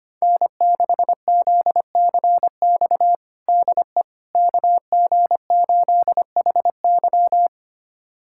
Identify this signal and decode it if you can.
Morse code, heard as a keyed tone, about 25 wpm, 700 Hz. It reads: N6ZCX DE KG8HY
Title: USA Callsign DE Callsign 25wpm